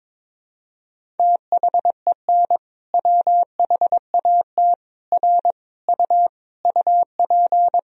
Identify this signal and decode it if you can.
Morse code, heard as a keyed tone, about 22 wpm, 700 Hz. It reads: THEN WHAT R U UP
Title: Sentences from Top 100 Words and Abbreviations 22wpm